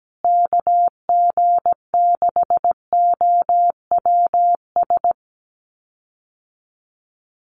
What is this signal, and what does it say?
Morse code, heard as a keyed tone, about 17 wpm, 700 Hz. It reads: KG6OWS
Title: USA Call Signs Repeated 3x 17wpm